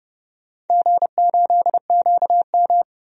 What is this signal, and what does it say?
Morse code, heard as a keyed tone, about 30 wpm, 700 Hz. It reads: G8QM
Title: DXCC-100 Call Signs - 4 Characters - 30wpm